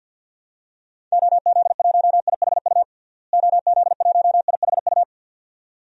Morse code, heard as a keyed tone, about 53 wpm, 700 Hz. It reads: OZ1IHU OZ1IHU
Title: DXCC-100 Call Signs - 6 Characters - Repeated 2x - 50wpm